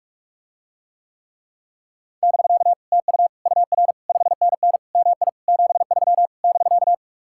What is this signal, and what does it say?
Morse code, heard as a keyed tone, about 50 wpm, 700 Hz. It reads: <BK> TU UR 5NN MI 73 <BK>